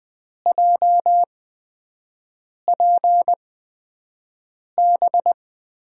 Morse code, heard as a keyed tone, about 20 wpm, spaced as for 5 wpm, 700 Hz. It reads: JPB
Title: USA Call Signs - 6 Characters - 8x Character Spacing - 20wpm